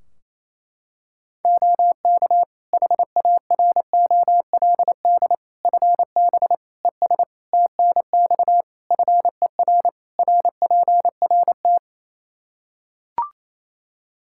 Morse code, heard as a keyed tone, about 28 wpm, 700 Hz. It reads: OK HAROLD FB ES TNX FER RPRT
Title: Practice QSO - Exchange 2 - Info - 28wpm